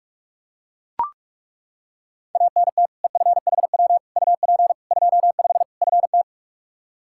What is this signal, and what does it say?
Morse code, heard as a keyed tone, about 41 wpm, 700 Hz. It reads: ANT EFHW UP 25 FT
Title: Practice QSO - Exchange 2 - Info - 45wpm